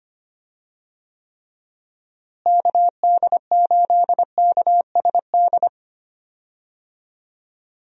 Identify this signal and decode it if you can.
Morse code, heard as a keyed tone, about 25 wpm, 700 Hz. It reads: KD8KSD
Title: USA Call Signs - 6 Characters - Repeated 2x - 25wpm